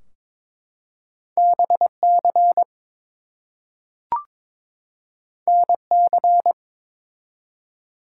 Morse code, heard as a keyed tone, about 22 wpm, 700 Hz. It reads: BC NC
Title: US States and Canadian Provinces 22wpm